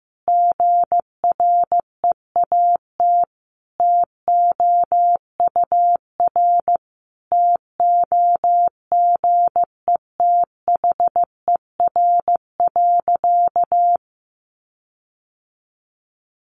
Morse code, heard as a keyed tone, about 15 wpm, 700 Hz. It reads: GREAT TOUR TOGETHER.
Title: Sentences from Top 1400 Words 15wpm